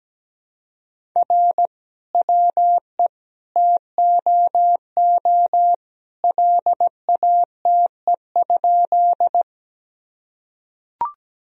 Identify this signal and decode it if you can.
Morse code, heard as a keyed tone, about 17 wpm, 700 Hz. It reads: R WE TOO LATE?
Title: Sentences from Top 300 Words and Abbreviations 17wpm